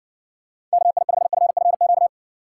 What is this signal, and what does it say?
Morse code, heard as a keyed tone, about 60 wpm, 700 Hz. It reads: KI5FFQ